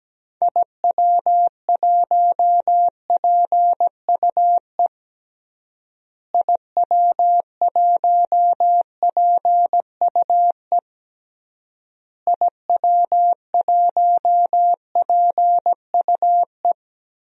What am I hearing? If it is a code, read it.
Morse code, heard as a keyed tone, about 17 wpm, 700 Hz. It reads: IW1PUE IW1PUE IW1PUE
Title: DXCC-100 Call Signs - 6 Characters - Repeated 3x - 17wpm